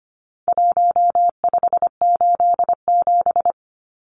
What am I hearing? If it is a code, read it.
Morse code, heard as a keyed tone, about 25 wpm, 700 Hz. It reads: 1587
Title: Numbers - 4-Digit 25wpm